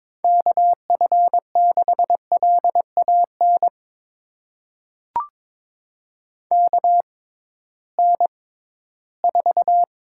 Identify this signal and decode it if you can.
Morse code, heard as a keyed tone, about 22 wpm, spaced as for 14 wpm, 700 Hz. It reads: KF6LAN KN4
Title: USA Call Signs - 6 Characters - 6x Character Spacing - 22wpm